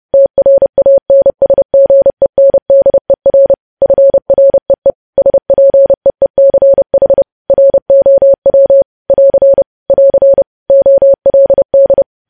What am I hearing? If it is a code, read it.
Morse code, heard as a keyed tone, about 30 wpm, 550 Hz. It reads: TRANSGENDER FREE SPEECH ROW <AR> <AR> OLD
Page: news headlines in morse code at 30 wpm